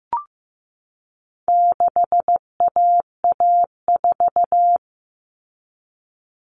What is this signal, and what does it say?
Morse code, heard as a keyed tone, about 15 wpm, 700 Hz. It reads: 6AA4
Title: All Cut Numbers - 4-Digit 15wpm